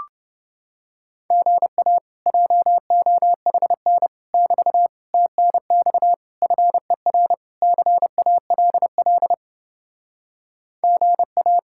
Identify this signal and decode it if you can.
Morse code, heard as a keyed tone, about 31 wpm, 700 Hz. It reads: GA JOHN <BT> TNX FER CALL GA